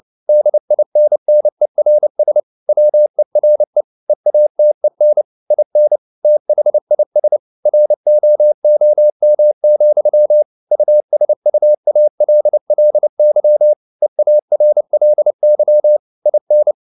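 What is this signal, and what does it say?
Morse code, heard as a keyed tone, about 29 wpm, 600 Hz. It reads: DINNERS WERE EATEN IN THIS ROOM, USUALLY EARLY IN
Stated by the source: Morse code at 29 wpm, sending Ken Follett's novel Never in English, from CW med Gnister.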